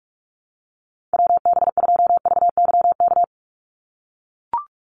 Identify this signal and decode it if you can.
Morse code, heard as a keyed tone, about 44 wpm, 700 Hz. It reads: WB2VYK
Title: Call Signs - 45wpm